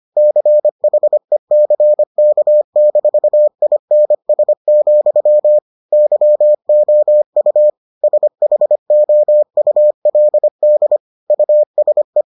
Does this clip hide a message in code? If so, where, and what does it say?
Yes: Morse code from 0.2 to 12.2 s, heard as a keyed tone, about 25 wpm, 600 Hz. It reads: CHECK-INS, YOU SHOULD USE